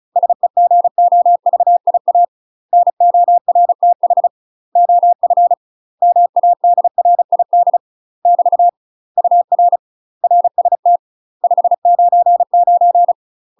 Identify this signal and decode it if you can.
Morse code, heard as a keyed tone, about 36 wpm, 700 Hz. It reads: SEGOVIA NORTH OF MADRID <BT> UR RST 599
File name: UK MASTER QSOs TRAINING / 35WPM / UK MASTER QSOs 0033